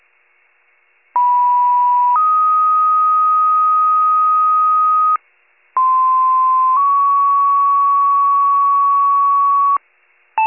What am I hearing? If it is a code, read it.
Two-tone page, heard as a one-second tone, then a three-second tone, about 980 then 1250 Hz, about 1040 then 1120 Hz. It is heard from 1.2 to 9.8 s.